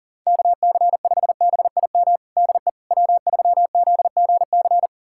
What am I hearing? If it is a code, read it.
Morse code, heard as a keyed tone, about 40 wpm, 700 Hz. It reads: KC5BIM DE W3ZGC
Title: USA Callsign DE Callsign 40wpm